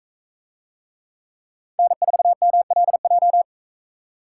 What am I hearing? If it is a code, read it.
Morse code, heard as a keyed tone, about 40 wpm, 700 Hz. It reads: N4MLJ